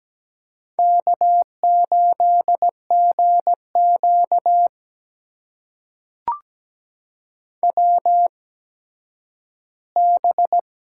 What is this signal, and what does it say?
Morse code, heard as a keyed tone, about 17 wpm, spaced as for 10 wpm, 700 Hz. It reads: K8GQ WB
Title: USA Call Signs - 4 Characters - 8x Character Spacing - 17wpm